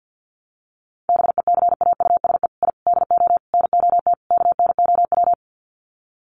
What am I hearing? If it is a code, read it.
Morse code, heard as a keyed tone, about 50 wpm, 700 Hz. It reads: BECAUSE I DO NOT KNOW
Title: Sets of 5 Words Encore2 50wpm